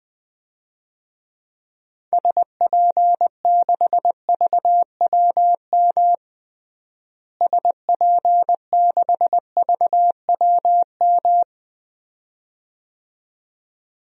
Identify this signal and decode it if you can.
Morse code, heard as a keyed tone, about 21 wpm, 700 Hz. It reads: SP6VWM SP6VWM
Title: DXCC-100 Call Signs - 6 Characters - Repeated 2x - 20wpm